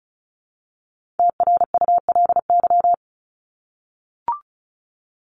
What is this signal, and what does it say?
Morse code, heard as a keyed tone, about 34 wpm, 700 Hz. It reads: TRULY
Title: LY Suffix - Top 100 Words with Base Words 35wpm